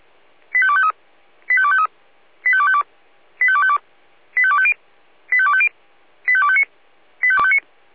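Five-tone selective call (ZVEI), heard as quick runs of five tones, tones of about 70 ms. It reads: 85253 85253 85252 85252 85250 85250 85259 85259
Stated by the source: ZVEI five-tone selective call received by SDR.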